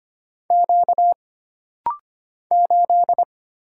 Morse code, heard as a keyed tone, about 25 wpm, 700 Hz. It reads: Q 8
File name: Single Character (Letter:Number:Punctuation:Prosign) - Rapid-Fire - 25wpm